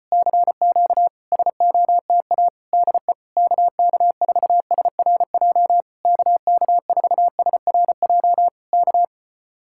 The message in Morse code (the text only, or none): CQ SOTA DE KK4SRJ KK4SRJ K